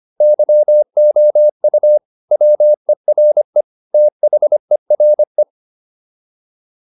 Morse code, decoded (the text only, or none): YOU WERE THERE